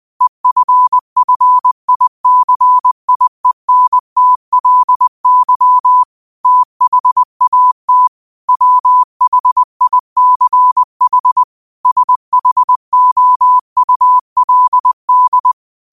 Morse code, heard as a keyed tone, about 20 wpm, 1000 Hz. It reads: EFFICIENTLY THAT WHICH SHOULD